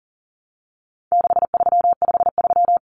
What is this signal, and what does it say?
Morse code, heard as a keyed tone, about 40 wpm, 700 Hz. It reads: 6353
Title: Numbers - 4-Digit 40wpm